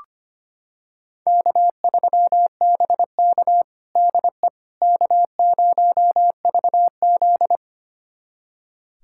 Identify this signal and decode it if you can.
Morse code, heard as a keyed tone, about 25 wpm, 700 Hz. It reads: K3BK DE K0VZ
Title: USA Callsign DE Callsign - 4 Characters - 25wpm